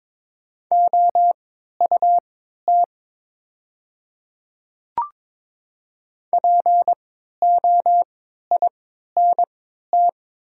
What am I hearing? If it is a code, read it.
Morse code, heard as a keyed tone, about 23 wpm, spaced as for 13 wpm, 700 Hz. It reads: OUT POINT
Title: 3x Character Spacing - Top 200 Words - 22wpm